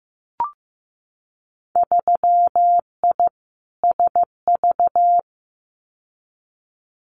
Morse code, heard as a keyed tone, about 15 wpm, 700 Hz. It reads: 3I SV
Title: Winter Field Day Exchanges Encore 15wpm